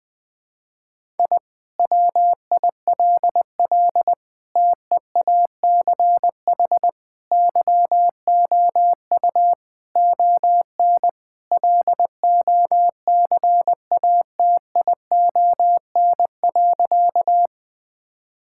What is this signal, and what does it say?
Morse code, heard as a keyed tone, about 20 wpm, 700 Hz. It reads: I WILL TEACH YOU ON LOCATION.
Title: Sentences from Top 1000 Words - Review Part 6 - 20wpm